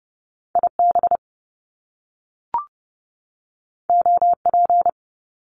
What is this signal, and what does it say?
Morse code, heard as a keyed tone, about 30 wpm, 700 Hz. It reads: IB OP
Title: Top Two Letter Combinations in Words 30wpm